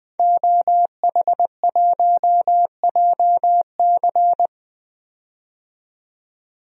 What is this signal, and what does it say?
Morse code, heard as a keyed tone, about 20 wpm, 700 Hz. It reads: OH1JC
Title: DXCC-100 Call Signs - 5 Characters - 20wpm